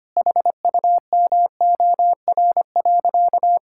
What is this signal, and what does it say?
Morse code, heard as a keyed tone, about 25 wpm, 700 Hz. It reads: HUMOR.
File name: practice / 25wpm / Sentences with 2-Word Phrases - 25wpm